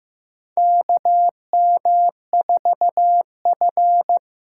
Morse code, heard as a keyed tone, about 15 wpm, 700 Hz. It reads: KM4F
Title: USA Call Signs - 6 Characters - 6x Character Spacing - 15wpm